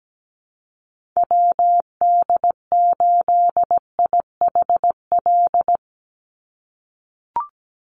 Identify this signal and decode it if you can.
Morse code, heard as a keyed tone, about 17 wpm, 700 Hz. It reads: WD8IHL
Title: USA Call Signs 17wpm